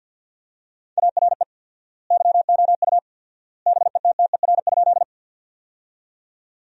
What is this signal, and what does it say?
Morse code, heard as a keyed tone, about 50 wpm, 700 Hz. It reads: ARE YOU BETTER?
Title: 4x Word-Spacing - Sentences from Top 400 Words - 50wpm